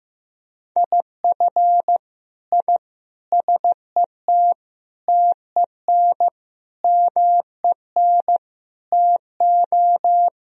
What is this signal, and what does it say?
Morse code, heard as a keyed tone, about 15 wpm, 700 Hz. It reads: IF I SET TEN MEN TO
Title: Sentences from Top 1000 Words - Review Part 2 - 15wpm